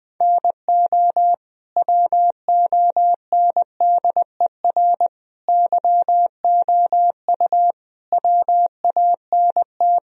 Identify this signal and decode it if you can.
Morse code, heard as a keyed tone, about 21 wpm, 700 Hz. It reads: NO WONDER YOU WANT
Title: Sentences from Top 1000 Words - Review Part 3 - 20wpm